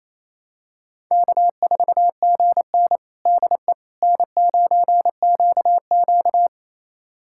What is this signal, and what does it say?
Morse code, heard as a keyed tone, about 28 wpm, 700 Hz. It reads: K4GN DE N9QQ
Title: USA Callsign DE Callsign - 4 Characters - 28wpm